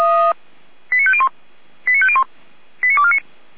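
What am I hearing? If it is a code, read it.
Five-tone selective call (ZVEI), heard as quick runs of five tones, tones of about 70 ms. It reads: 89501 89501 89249
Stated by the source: SDR recording of ZVEI five-tone selective call 89123, from the Feuerwehr Orlamünde (fire brigade) server.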